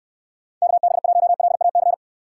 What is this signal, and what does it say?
Morse code, heard as a keyed tone, about 67 wpm, 700 Hz. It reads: KB1LAC